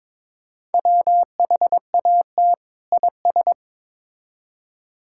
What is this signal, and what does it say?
Morse code, heard as a keyed tone, about 21 wpm, 700 Hz. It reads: WHAT IS
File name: Build A Phrase - 2-Word Phrases - 22wpm